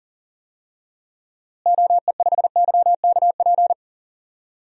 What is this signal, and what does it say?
Morse code, heard as a keyed tone, about 37 wpm, 700 Hz. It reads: OE5YKP
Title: DXCC-100 Call Signs - 6 Characters - Repeated 2x - 40wpm